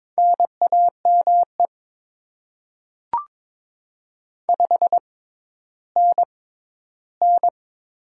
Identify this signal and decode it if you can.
Morse code, heard as a keyed tone, about 22 wpm, spaced as for 13 wpm, 690 Hz. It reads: NAME 5NN
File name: Top 20 QSO Elements - 6x Character Spacing - 22wpm